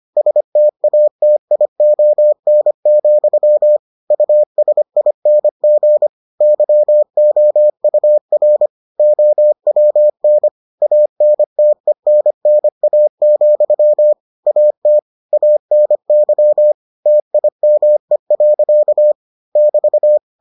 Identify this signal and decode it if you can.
Morse code, heard as a keyed tone, about 25 wpm, 600 Hz. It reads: STATION, USING YOUR OWN ANTENNA, AT ANY TIME. <BT>